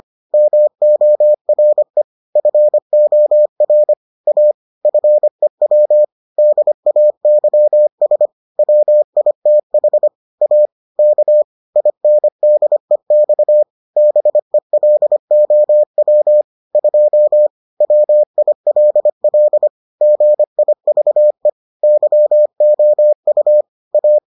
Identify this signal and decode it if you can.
Morse code, heard as a keyed tone, about 25 wpm, 600 Hz. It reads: MORE FOR A FEW DAYS WITH A K INDEX BELOW 2 WILL GIVE YOU A